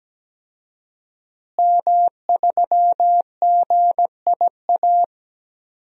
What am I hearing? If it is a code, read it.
Morse code, heard as a keyed tone, about 17 wpm, 700 Hz. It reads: M3GIA
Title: DXCC-100 Call Signs - 5 Characters - Repeated 2x - 17wpm